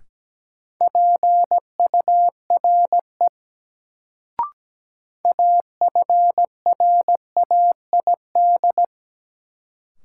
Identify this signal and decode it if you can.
Morse code, heard as a keyed tone, about 17 wpm, 700 Hz. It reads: PURE AFRAID
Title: Sentences from Top 1500 Words - New Word Drill Alt - 17wpm